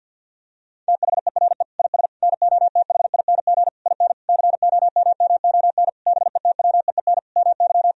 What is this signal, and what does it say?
Morse code, heard as a keyed tone, about 50 wpm, 700 Hz. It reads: THERE IS NOTHING IN COMMON BETWEEN MY
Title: Sentences from Top 1000 Words - Review Part 3 - 50wpm